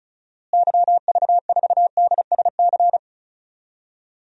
Morse code, heard as a keyed tone, about 34 wpm, 700 Hz. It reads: YV4DSC
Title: DXCC-100 Call Signs - 6 Characters - Repeated 2x - 35wpm